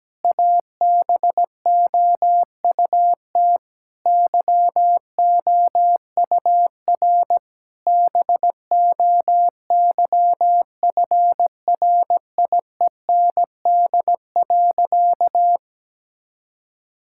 Morse code, heard as a keyed tone, about 17 wpm, 700 Hz. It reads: ABOUT YOUR BOYFRIEND.
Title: Sentences from Top 1000 Words - Review Part 5 - 17wpm